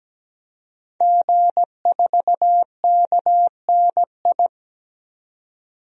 Morse code, heard as a keyed tone, about 17 wpm, 700 Hz. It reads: G4KNI
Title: DXCC-100 Call Signs - 5 Characters - 17wpm